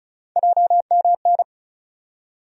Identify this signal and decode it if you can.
Morse code, heard as a keyed tone, about 36 wpm, 700 Hz. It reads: JMN
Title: USA Call Sign Postfixes - 3 Letters 35wpm